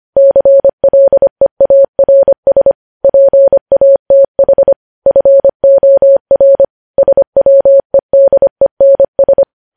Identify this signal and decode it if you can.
Morse code, heard as a keyed tone, about 25 wpm, 550 Hz. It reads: CLEARS PATH FOR SWEDENS
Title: News Headlines at 25